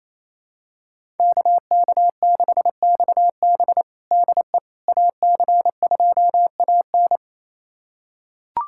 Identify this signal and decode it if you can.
Morse code, heard as a keyed tone, about 28 wpm, 700 Hz. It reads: KK6XB DE AC2AN
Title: USA Callsign DE Callsign - 5 Characters - 28wpm